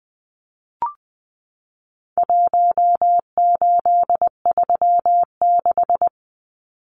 Morse code, heard as a keyed tone, about 20 wpm, 700 Hz. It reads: 1836
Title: Numbers - Contest Style - 4-Digit 20wpm